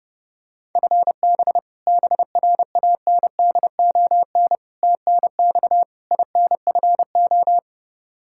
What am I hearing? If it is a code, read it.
Morse code, heard as a keyed tone, about 32 wpm, 700 Hz. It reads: FB BRANDON TNX INFO
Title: Practice QSO - Exchange 3 - More Info - 30wpm